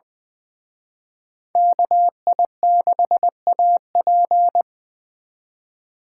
Morse code, heard as a keyed tone, about 24 wpm, 700 Hz. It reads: KI6AP
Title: Speed Racing - USA 2x2 Call Signs 20wpm